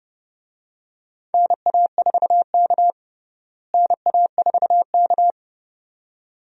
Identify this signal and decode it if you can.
Morse code, heard as a keyed tone, about 30 wpm, 700 Hz. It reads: NA4K NA4K
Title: USA Call Signs - 4 Characters - Repeated 2x - 30wpm